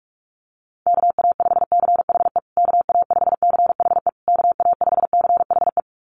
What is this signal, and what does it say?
Morse code, heard as a keyed tone, about 44 wpm, 700 Hz. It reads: KA5CHE KA5CHE KA5CHE
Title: USA Call Signs Repeated 3x 45wpm